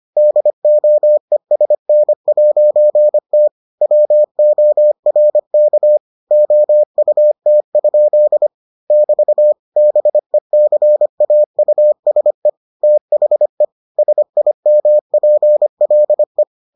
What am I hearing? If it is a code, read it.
Morse code, heard as a keyed tone, about 25 wpm, 600 Hz. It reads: DOESN'T WORK OUT? <BT> BECAUSE THE SIMPLE